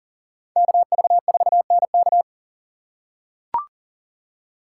Morse code, heard as a keyed tone, about 40 wpm, 700 Hz. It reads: KV4NK
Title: USA Call Signs - 5 Characters - 2x Character Spacing - 40wpm